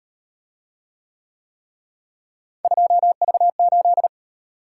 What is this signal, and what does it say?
Morse code, heard as a keyed tone, about 34 wpm, 700 Hz. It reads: 2V8